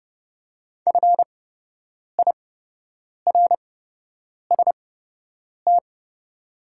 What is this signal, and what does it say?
Morse code, heard as a keyed tone, about 30 wpm, spaced as for 8 wpm, 700 Hz. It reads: FIRST
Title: 8x Character Spacing - Top 100 Words - 30wpm